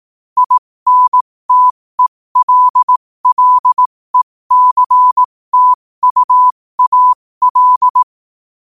Morse code, heard as a keyed tone, about 18 wpm, spaced as for 15 wpm, 1000 Hz. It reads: INTELLECTUAL
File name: qotdfiles / 15wpm